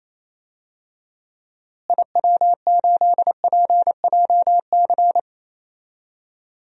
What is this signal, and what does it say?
Morse code, heard as a keyed tone, about 28 wpm, 700 Hz. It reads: IW8PJC
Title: DXCC-100 Call Signs - 6 Characters - Repeated 2x - 28wpm